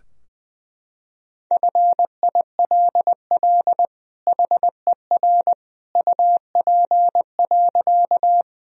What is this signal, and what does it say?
Morse code, heard as a keyed tone, about 21 wpm, 700 Hz. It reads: FILL HER UP.
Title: Sentences from Top 1000 Words - Review Part 3 - 20wpm